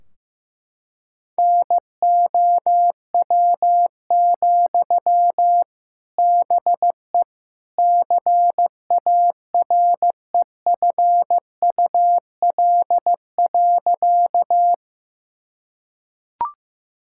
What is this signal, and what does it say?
Morse code, heard as a keyed tone, about 15 wpm, 700 Hz. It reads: NOW, BE CAREFUL.
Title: Sentences from Top 1600 Words 15wpm